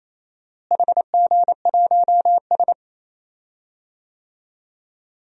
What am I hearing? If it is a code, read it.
Morse code, heard as a keyed tone, about 28 wpm, 700 Hz. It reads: HG1S
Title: DXCC-100 Call Signs - 4 Characters - Repeated 2x - 28wpm